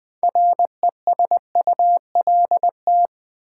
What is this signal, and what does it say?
Morse code, heard as a keyed tone, about 20 wpm, 700 Hz. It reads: RESULT
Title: 6 Letter Words - First 3 Letters Given - 20wpm